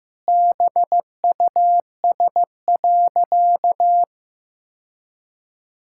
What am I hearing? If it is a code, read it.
Morse code, heard as a keyed tone, about 15 wpm, 700 Hz. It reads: BUS.
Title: Sentences from Top 1000 Words - Review Part 6 - 15wpm